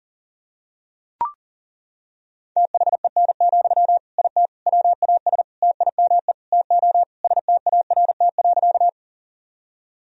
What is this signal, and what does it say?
Morse code, heard as a keyed tone, about 40 wpm, 700 Hz. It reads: THEN, IT WAS TIME TO START.
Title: Sentences from Top 1000 Words - Review Part 1 - 40wpm